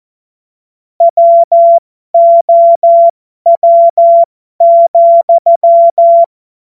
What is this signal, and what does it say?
Morse code, heard as a keyed tone, about 14 wpm, spaced as for 12 wpm, 680 Hz. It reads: WOW,
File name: Morsekurs / Witze